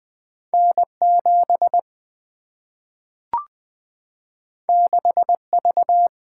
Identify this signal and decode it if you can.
Morse code, heard as a keyed tone, about 20 wpm, 700 Hz. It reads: N 7 6 V
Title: Single-Letter Single-Number 20wpm